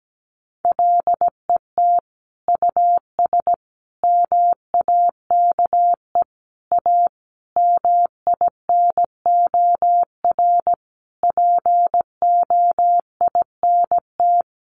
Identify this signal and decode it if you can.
Morse code, heard as a keyed tone, about 17 wpm, 700 Hz. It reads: LET US MAKE A MINOR POINT
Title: Sentences from Top 1500 Words 17wpm